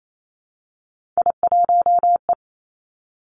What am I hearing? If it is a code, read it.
Morse code, heard as a keyed tone, about 28 wpm, 700 Hz. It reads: I1E
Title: Three Challenging Letter-Number Combinations 28wpm